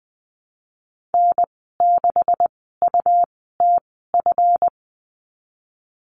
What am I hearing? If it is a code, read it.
Morse code, heard as a keyed tone, about 20 wpm, spaced as for 15 wpm, 700 Hz. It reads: N6UTF
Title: 179 Lesson 29g - Slash - Call Signs 20wpm